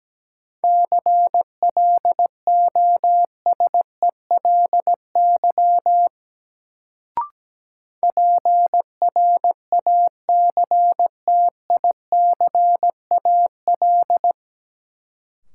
Morse code, heard as a keyed tone, about 17 wpm, 700 Hz. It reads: CLOSELY PRACTICAL
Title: Sentences from Top 1400 Words - New Word Drill Alt - 17wpm